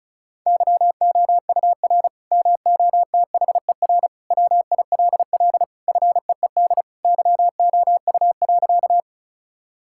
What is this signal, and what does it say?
Morse code, heard as a keyed tone, about 36 wpm, 700 Hz. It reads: YOUR MOTHER WILL FEED YOU.
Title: Sentences from Top 1600 Words 35wpm